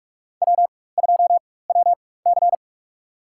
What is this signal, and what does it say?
Morse code, heard as a keyed tone, about 46 wpm, spaced as for 21 wpm, 700 Hz. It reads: W2WC
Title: USA Call Signs - 4 Characters - 4x Character Spacing - 45wpm